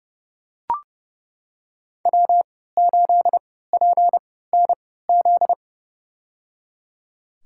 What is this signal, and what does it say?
Morse code, heard as a keyed tone, about 30 wpm, spaced as for 17 wpm, 700 Hz. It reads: W8PNZ
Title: USA Call Signs - 5 Characters - 3x Character Spacing - 30wpm